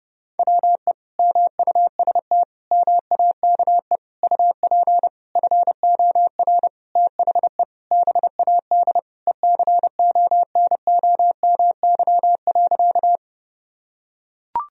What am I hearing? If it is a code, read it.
Morse code, heard as a keyed tone, about 32 wpm, 700 Hz. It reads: WE MUST MAKE UP FOR THE BAD ECONOMY.
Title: Sentences from Top 1000 Words - Review Part 3 - 30wpm